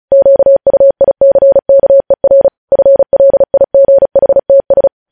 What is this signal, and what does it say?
Morse code, heard as a keyed tone, about 36 wpm, 550 Hz. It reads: QUICKER FLIGHTS
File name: podcast / News headlines-1x-35x35